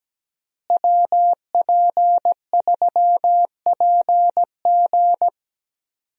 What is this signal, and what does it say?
Morse code, heard as a keyed tone, about 17 wpm, 700 Hz. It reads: WP3PG
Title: USA Call Signs - 5 Characters - Repeated 3x - 17wpm